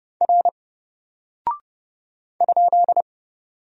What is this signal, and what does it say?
Morse code, heard as a keyed tone, about 30 wpm, 700 Hz. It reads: R ?
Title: Single Character (Letter:Number:Punctuation:Prosign) 30wpm